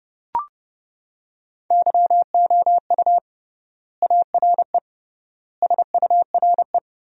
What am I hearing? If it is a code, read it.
Morse code, heard as a keyed tone, about 30 wpm, 700 Hz. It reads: YOU ARE SURE
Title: 3x Word-Spacing - Sentences from Top 300 Words - 30wpm